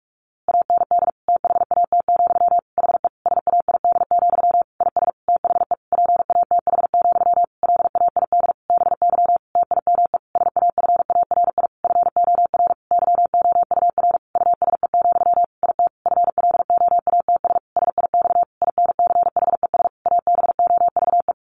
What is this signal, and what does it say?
Morse code, heard as a keyed tone, about 44 wpm, 700 Hz. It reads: AND THAT, HE SAID, IS THE PATH, LAID BY TIME SAFARI FOR YOUR USE, IT FLOATS SIX INCHES ABOVE
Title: A Sound Of Thunder 45wpm